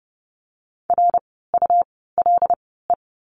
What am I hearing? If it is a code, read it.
Morse code, heard as a keyed tone, about 28 wpm, spaced as for 17 wpm, 700 Hz. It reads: RULE
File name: Lesson 17f - Letter u - Words 30wpm